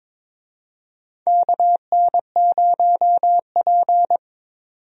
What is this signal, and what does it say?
Morse code, heard as a keyed tone, about 22 wpm, 700 Hz. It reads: KN0P
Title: USA Call Signs - 4 Characters - 22wpm